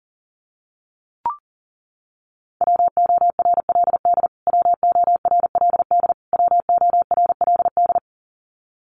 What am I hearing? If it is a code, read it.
Morse code, heard as a keyed tone, about 40 wpm, 700 Hz. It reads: WORLD WORLD WORLD
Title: Top 200 Words Repeated 40wpm